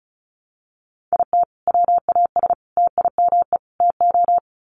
Morse code, heard as a keyed tone, about 34 wpm, 700 Hz. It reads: IT WAS TIME TO
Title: Sets of 4 Words 35wpm